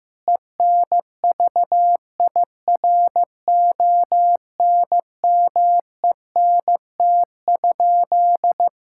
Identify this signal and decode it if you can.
Morse code, heard as a keyed tone, about 15 wpm, 700 Hz. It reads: ENVIRONMENT?
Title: Sentences from Top 1000 Words - Review Part 6 - 15wpm